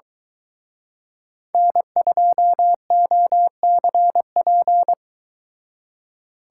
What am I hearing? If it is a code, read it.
Morse code, heard as a keyed tone, about 23 wpm, 700 Hz. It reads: N2OCP